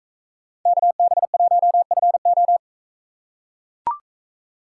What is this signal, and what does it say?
Morse code, heard as a keyed tone, about 44 wpm, 700 Hz. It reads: KB1FO